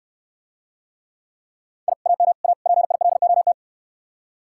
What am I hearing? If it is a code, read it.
Morse code, heard as a keyed tone, about 69 wpm, 700 Hz. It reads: I AM A PILOT